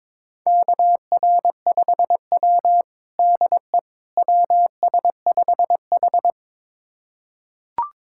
Morse code, heard as a keyed tone, about 22 wpm, 700 Hz. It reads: KR5W DE WS5H
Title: USA Callsign DE Callsign - 4 Characters - 22wpm